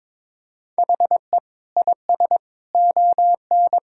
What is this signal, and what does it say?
Morse code, heard as a keyed tone, about 22 wpm, 700 Hz. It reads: HE IS ON
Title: Sentences from Top 1000 Words - Review Part 4 - 22wpm